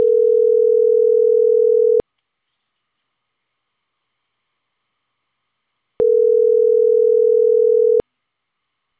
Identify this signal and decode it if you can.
Ringback tone (North America), 440 + 480 Hz.